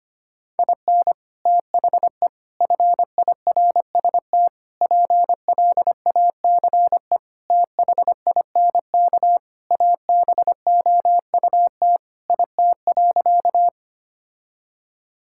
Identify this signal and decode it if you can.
Morse code, heard as a keyed tone, about 25 wpm, 700 Hz. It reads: IN THE FIRST PLACE THINK ABOUT IT.
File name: practice / 25wpm / Sentences with 3-Word Phrases - 25wpm